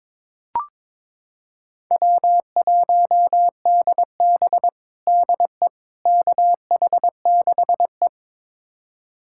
Morse code, heard as a keyed tone, about 22 wpm, 700 Hz. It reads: W1DB DE KH6E
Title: USA Callsign DE Callsign - 4 Characters - 22wpm